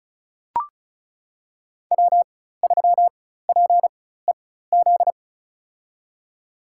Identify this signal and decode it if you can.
Morse code, heard as a keyed tone, about 34 wpm, spaced as for 16 wpm, 700 Hz. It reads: W3PEZ
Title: USA Call Signs - 4x Character Spacing - 35wpm